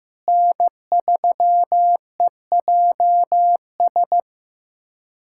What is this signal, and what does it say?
Morse code, heard as a keyed tone, about 16 wpm, 700 Hz. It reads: N3EJS
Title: USA Callsign DE Callsign 15wpm